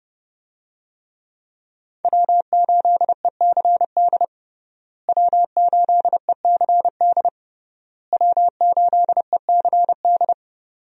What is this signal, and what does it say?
Morse code, heard as a keyed tone, about 30 wpm, 700 Hz. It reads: W8ECD W8ECD W8ECD
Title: USA Call Signs - 5 Characters - Repeated 3x - 30wpm